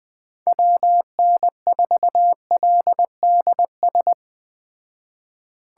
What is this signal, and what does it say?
Morse code, heard as a keyed tone, about 19 wpm, 700 Hz. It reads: WN4LDS